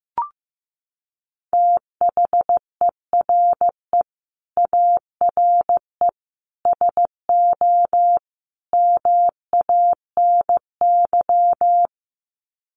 Morse code, heard as a keyed tone, about 15 wpm, 700 Hz. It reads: THERE ARE SO MANY
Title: Sets of 4 Words 15wpm